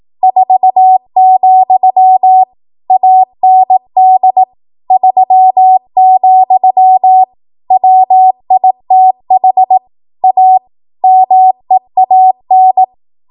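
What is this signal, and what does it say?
Morse code, heard as a keyed tone, about 18 wpm, 750 Hz. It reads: 4, AND 3, WITH A MEAN
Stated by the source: ARRL 18 wpm Morse code practice archive transmission.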